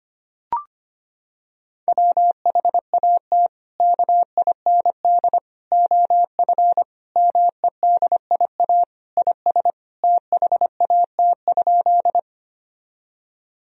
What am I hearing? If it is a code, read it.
Morse code, heard as a keyed tone, about 25 wpm, 700 Hz. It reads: WHAT KIND OF MEDIA IS THAT?
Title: Sentences from Top 1000 Words - Review Part 4 - 25wpm